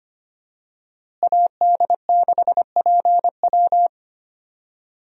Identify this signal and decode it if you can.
Morse code, heard as a keyed tone, about 25 wpm, 700 Hz. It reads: AD6PW
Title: USA Call Signs - 5 Characters - Repeated 2x - 25wpm